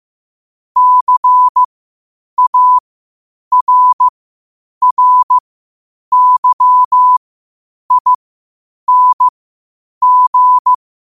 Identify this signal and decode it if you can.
Morse code, heard as a keyed tone, about 15 wpm, spaced as for 8 wpm, 1000 Hz. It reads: CARRYING